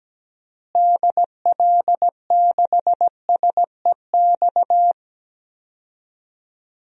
Morse code, heard as a keyed tone, about 17 wpm, 700 Hz. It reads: DL6SEX
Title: DXCC-100 Call Signs - 6 Characters - Repeated 2x - 17wpm